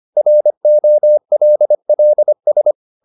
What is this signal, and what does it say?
Morse code, heard as a keyed tone, about 25 wpm, 600 Hz. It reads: ROLLS
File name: the old man and the sea part 1 25wpm